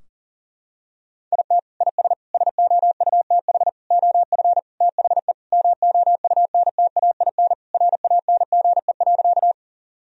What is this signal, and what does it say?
Morse code, heard as a keyed tone, about 40 wpm, 700 Hz. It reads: IT IS SOUTH OF THE MOUNTAIN RANGE.
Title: Sentences from Top 1000 Words - Review Part 4 - 40wpm